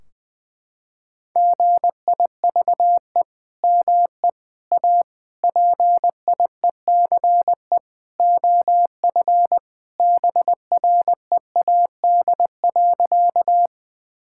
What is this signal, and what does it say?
Morse code, heard as a keyed tone, about 20 wpm, 700 Hz. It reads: GIVE ME A PIECE OF BREAD.
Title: Sentences from Top 1600 Words 20wpm